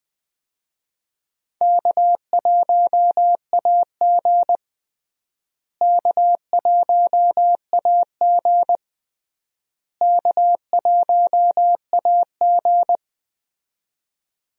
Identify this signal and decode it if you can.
Morse code, heard as a keyed tone, about 20 wpm, 700 Hz. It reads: K1AG K1AG K1AG
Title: USA Call Signs - 4 Characters - Repeated 3x - 20wpm